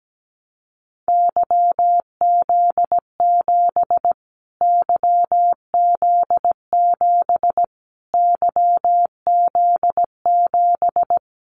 Morse code, heard as a keyed tone, about 17 wpm, 700 Hz. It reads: YZ7 YZ7 YZ7
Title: Callsign Prefixes for Top 25 DXCC - Repeated 3x - 17wpm